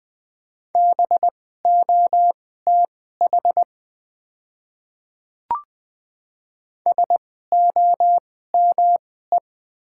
Morse code, heard as a keyed tone, about 20 wpm, spaced as for 15 wpm, 700 Hz. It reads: BOTH SOME
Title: 2x Character Spacing - Top 300 Words - 20wpm